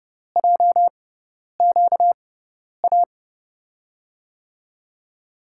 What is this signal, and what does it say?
Morse code, heard as a keyed tone, about 30 wpm, spaced as for 10 wpm, 700 Hz. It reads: JQA
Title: USA Call Signs - 6x Character Spacing - 30wpm